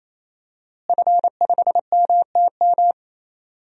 Morse code, heard as a keyed tone, about 27 wpm, 700 Hz. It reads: F5MTM
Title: DXCC-100 Call Signs - 5 Characters - Repeated 3x - 28wpm